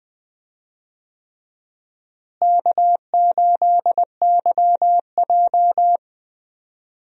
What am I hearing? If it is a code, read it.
Morse code, heard as a keyed tone, about 20 wpm, 700 Hz. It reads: K8YJ
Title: USA Call Signs - 4 Characters - Repeated 3x - 20wpm